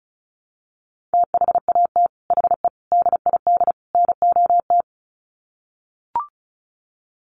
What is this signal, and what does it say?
Morse code, heard as a keyed tone, about 34 wpm, 700 Hz. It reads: THAT HE DID NOT
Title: Sets of 4 Words 35wpm